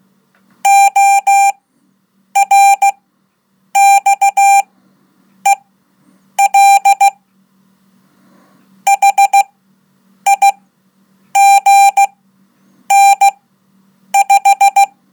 Morse code, heard as a keyed tone, about 15 wpm, spaced as for 8 wpm, 790 Hz. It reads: ORXEL HIGN5
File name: Code / 5min-1